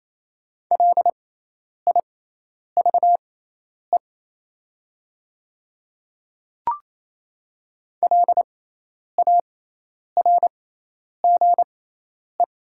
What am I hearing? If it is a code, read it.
Morse code, heard as a keyed tone, about 28 wpm, spaced as for 10 wpm, 700 Hz. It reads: LIVE LARGE
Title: 6x Character Spacing - Top 300 Words - 28wpm